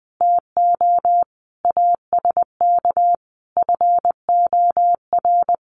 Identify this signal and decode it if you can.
Morse code, heard as a keyed tone, about 20 wpm, 700 Hz. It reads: TO ASK FOR
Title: Sentences from Top 1100 Words 20wpm